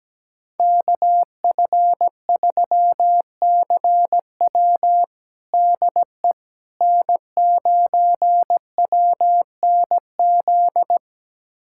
Morse code, heard as a keyed tone, about 17 wpm, 700 Hz. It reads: KF3CW DE N9WNZ
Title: USA Callsign DE Callsign - 5 Characters - 17wpm